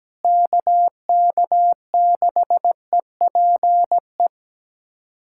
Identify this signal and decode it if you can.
Morse code, heard as a keyed tone, about 17 wpm, 700 Hz. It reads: KK6EPE
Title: USA Callsign DE Callsign 17wpm